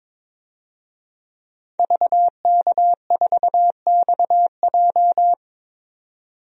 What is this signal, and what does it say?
Morse code, heard as a keyed tone, about 22 wpm, 700 Hz. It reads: VK4XJ
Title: DXCC-100 Call Signs - 5 Characters - 22wpm